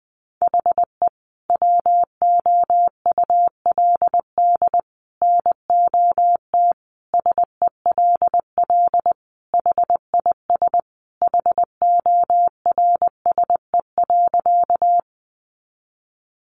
Morse code, heard as a keyed tone, about 20 wpm, 700 Hz. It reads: HE WOULD NOT SELL HIS HORSE.
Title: Sentences from Top 1100 Words 20wpm